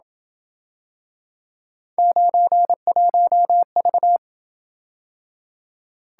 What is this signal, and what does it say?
Morse code, heard as a keyed tone, about 27 wpm, 700 Hz. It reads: 91V